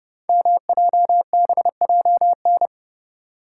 Morse code, heard as a keyed tone, about 30 wpm, 700 Hz. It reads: MJBJN
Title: Code Groups - 5 Letters - 30wpm